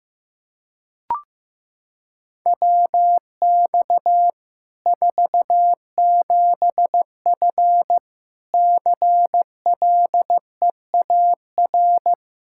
Morse code, heard as a keyed tone, about 15 wpm, 700 Hz. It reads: WX 47F CLEAR
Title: Practice QSO - Exchange 2 - Info - 15wpm